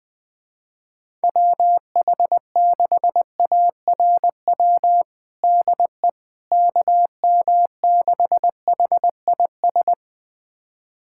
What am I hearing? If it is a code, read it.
Morse code, heard as a keyed tone, about 20 wpm, 700 Hz. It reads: WH6ARW DE KM6HIS
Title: USA Callsign DE Callsign - 6 Characters - 20wpm